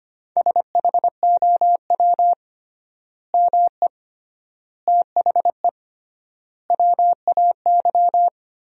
Morse code, heard as a keyed tone, about 25 wpm, 700 Hz. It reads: SHOW ME THE WAY
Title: 3x Word-Spacing - Sentences from Top 200 Words - 25wpm